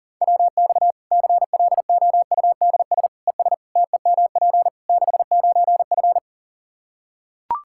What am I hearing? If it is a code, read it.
Morse code, heard as a keyed tone, about 40 wpm, 700 Hz. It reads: WX CLOUDS ES TEMP 69F
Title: K1USN SST QSO Practice 40wpm